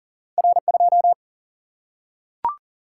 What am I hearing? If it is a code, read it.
Morse code, heard as a keyed tone, about 40 wpm, 700 Hz. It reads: R2
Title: Single-Letter Single-Number 40wpm